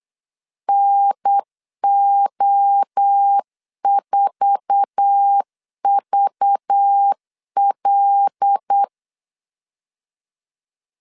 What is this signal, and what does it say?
Morse code, heard as a keyed tone, about 8 wpm, 800 Hz. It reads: NO4VL
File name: CW / DET / 1 9 1500C S C 8wpm 5653